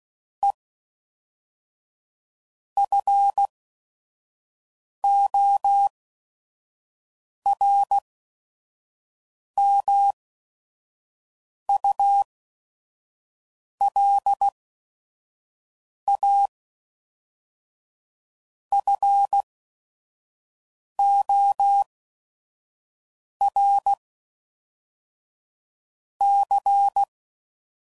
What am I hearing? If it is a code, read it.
Morse code, heard as a keyed tone, about 16 wpm, spaced as for 5 wpm, 800 Hz. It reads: E FORMULA FOR C
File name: CW Code Groups @5WPM / Technical Subjects / XC